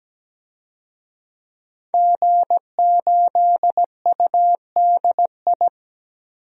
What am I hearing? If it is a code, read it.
Morse code, heard as a keyed tone, about 17 wpm, 700 Hz. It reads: G8UDI
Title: DXCC-100 Call Signs - 5 Characters - 17wpm